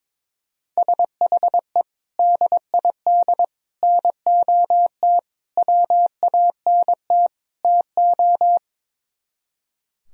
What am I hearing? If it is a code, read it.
Morse code, heard as a keyed tone, about 23 wpm, 700 Hz. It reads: SHE DID NOT WANT TO
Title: Speed-Racing - Sets of 5 Words 22wpm